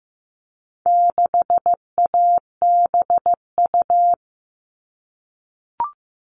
Morse code, heard as a keyed tone, about 15 wpm, 700 Hz. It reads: 6ABU
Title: All Cut Numbers - 4-Digit 15wpm